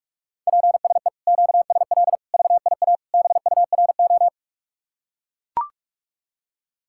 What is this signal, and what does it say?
Morse code, heard as a keyed tone, about 44 wpm, 700 Hz. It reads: PSE QSL VIA BURO